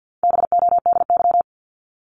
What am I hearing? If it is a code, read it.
Morse code, heard as a keyed tone, about 49 wpm, 700 Hz. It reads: BODY
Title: Top 500 Words 50wpm